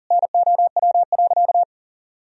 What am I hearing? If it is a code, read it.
Morse code, heard as a keyed tone, about 40 wpm, 700 Hz. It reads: NOW.
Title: Sentences from Top 1000 Words - Review Part 4 - 40wpm